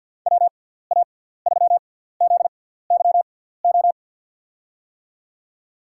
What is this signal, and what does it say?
Morse code, heard as a keyed tone, about 53 wpm, 700 Hz. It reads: W A 3 Z Y O
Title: USA Call Signs - 6 Characters - 6x Character Spacing - 50wpm